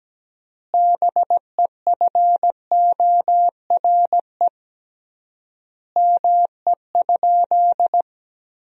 Morse code, heard as a keyed tone, about 17 wpm, 700 Hz. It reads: BEFORE ME?
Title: 3x Word-Spacing - Sentences from Top 200 Words - 17wpm